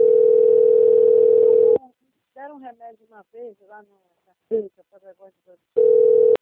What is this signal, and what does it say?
Ringback tone (North America), 440 + 480 Hz, with speech.
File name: out- - - - - .9945